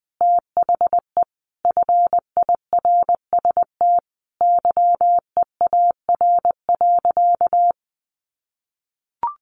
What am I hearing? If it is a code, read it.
Morse code, heard as a keyed tone, about 20 wpm, 700 Hz. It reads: THE FIRST YEAR.
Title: Sentences from Top 1300 Words 20wpm